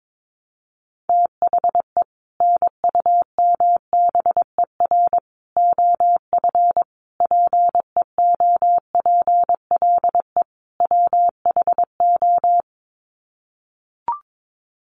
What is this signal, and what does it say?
Morse code, heard as a keyed tone, about 22 wpm, 700 Hz. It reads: THE NUMBER OF PEOPLE WHO
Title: Sets of 5 Words Encore 22wpm